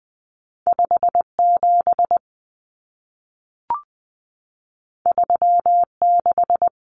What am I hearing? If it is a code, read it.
Morse code, heard as a keyed tone, about 20 wpm, 700 Hz. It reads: 57 36
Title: Common Cut Numbers - 2-Digit 20wpm